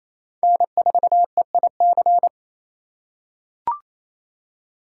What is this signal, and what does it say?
Morse code, heard as a keyed tone, about 28 wpm, 700 Hz. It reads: N4EIC
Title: USA Call Signs - 5 Characters - 4x Character Spacing - 28wpm